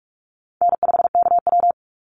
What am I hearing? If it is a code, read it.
Morse code, heard as a keyed tone, about 46 wpm, 700 Hz. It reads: N5KW
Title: Easy Call Signs - 45wpm